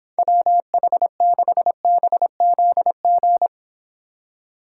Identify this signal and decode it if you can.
Morse code, heard as a keyed tone, about 26 wpm, 700 Hz. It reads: WH6BZG